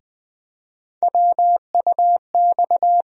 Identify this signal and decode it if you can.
Morse code, heard as a keyed tone, about 21 wpm, 700 Hz. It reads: WUX
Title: USA Call Sign Postfixes - 3 Letters 20wpm